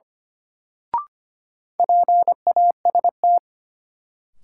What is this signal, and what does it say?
Morse code, heard as a keyed tone, about 25 wpm, 700 Hz. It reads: PAST
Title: Sentences from Top 600 Words - New Word Drill - 25wpm